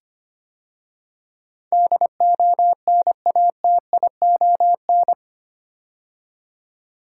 Morse code, heard as a keyed tone, about 25 wpm, 690 Hz. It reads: DONATION
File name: 8 Letter Words - First 3 Letters Given - 25wpm